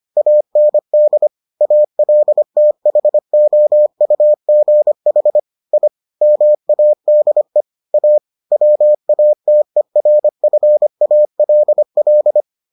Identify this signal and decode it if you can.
Morse code, heard as a keyed tone, about 25 wpm, 600 Hz. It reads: AND ALTHOUGH I MADE A WATERFALL